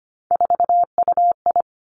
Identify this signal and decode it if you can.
Morse code, heard as a keyed tone, about 25 wpm, 700 Hz. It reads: 4UI
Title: Three Challenging Letter-Number Combinations 25wpm